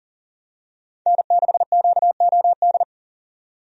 Morse code, heard as a keyed tone, about 40 wpm, 700 Hz. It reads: N6QOD
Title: USA Call Signs - 5 Characters - 40wpm